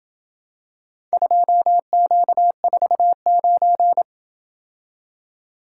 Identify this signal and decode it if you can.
Morse code, heard as a keyed tone, about 30 wpm, 700 Hz. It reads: 2Q49